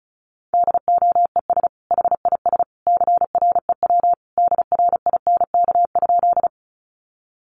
Morse code, heard as a keyed tone, about 34 wpm, 700 Hz. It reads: DOES HIS CREW DRINK?
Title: Sentences from Top 1500 Words 35wpm